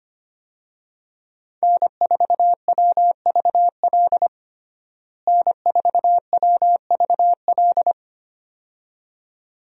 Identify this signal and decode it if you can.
Morse code, heard as a keyed tone, about 25 wpm, 700 Hz. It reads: N4WVL N4WVL
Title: USA Call Signs - 5 Characters - Repeated 2x - 25wpm